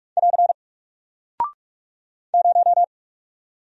Morse code, heard as a keyed tone, about 44 wpm, 700 Hz. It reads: <AR> 0
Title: Single Character (Letter:Number:Punctuation:Prosign) Alt 45wpm